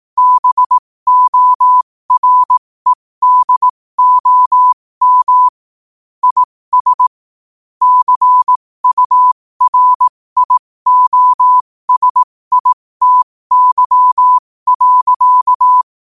Morse code, heard as a keyed tone, about 18 wpm, spaced as for 15 wpm, 1000 Hz. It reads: BOREDOM IS CURIOSITY.
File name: qotdfiles / 15wpm